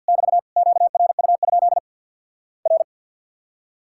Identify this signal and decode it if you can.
Morse code, heard as a keyed tone, about 50 wpm, 700 Hz. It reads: <BT> QRU? R
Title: Worked All Europe DX Contest - QTC Exchange - 50wpm